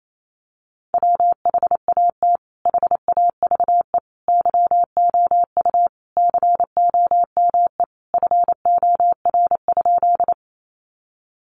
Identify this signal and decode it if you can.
Morse code, heard as a keyed tone, about 28 wpm, 700 Hz. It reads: WHAT HAVE YOU COME FOR?
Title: Sentences from Top 100 Words 28wpm